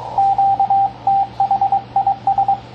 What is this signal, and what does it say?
Morse code, heard as a keyed tone, about 22 wpm, 790 Hz, with radio noise. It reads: QTH IS
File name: CW Learning Programs / sounds and personalities of cw and transmitters / 16 - K5XU Snags JH0INP on 10 Meters